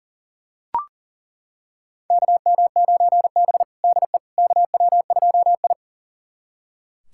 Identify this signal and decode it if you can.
Morse code, heard as a keyed tone, about 40 wpm, 700 Hz. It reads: KM9B DE KW2I
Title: USA Callsign DE Callsign - 4 Characters - 40wpm